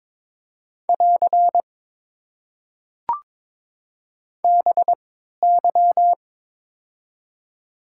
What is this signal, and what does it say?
Morse code, heard as a keyed tone, about 22 wpm, spaced as for 13 wpm, 700 Hz. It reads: <AR> BY
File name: Top 90 QSO Elements - 3x Character Spacing - 22wpm